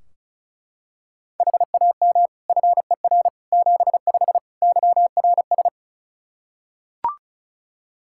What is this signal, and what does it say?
Morse code, heard as a keyed tone, about 36 wpm, 700 Hz. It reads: HAM FER 75 YRS
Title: Practice QSO - Exchange 3 - More Info - 35wpm